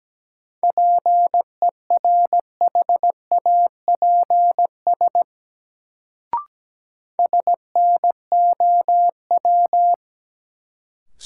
Morse code, heard as a keyed tone, about 17 wpm, 700 Hz. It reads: PERHAPS SNOW
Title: Sentences from Top 500 Words - New Word Drill - 17wpm